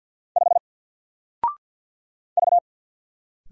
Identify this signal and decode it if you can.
Morse code, heard as a keyed tone, about 48 wpm, 700 Hz. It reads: 5 V
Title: Single Character (Letter:Number:Punctuation:Prosign) 50wpm